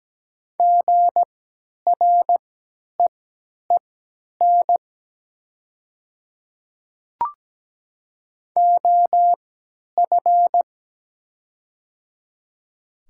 Morse code, heard as a keyed tone, about 17 wpm, spaced as for 10 wpm, 700 Hz. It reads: GREEN OF
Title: 3x Character Spacing - Top 500 Words - 17wpm